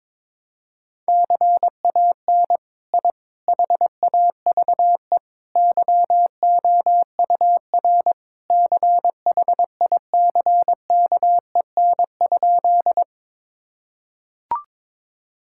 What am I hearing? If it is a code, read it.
Morse code, heard as a keyed tone, about 22 wpm, 700 Hz. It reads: CAN I HAVE YOUR CHICKEN?
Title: Sentences from Top 1000 Words - Review Part 5 - 22wpm